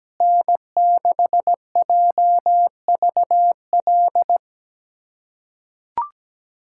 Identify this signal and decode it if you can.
Morse code, heard as a keyed tone, about 17 wpm, 700 Hz. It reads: N6JVL
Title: USA Callsign DE Callsign - 5 Characters - 17wpm